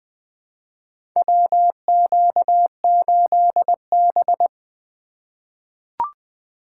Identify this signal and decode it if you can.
Morse code, heard as a keyed tone, about 20 wpm, 700 Hz. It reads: WQ8B
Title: USA Call Signs - 4 Characters - 3x Character Spacing - 20wpm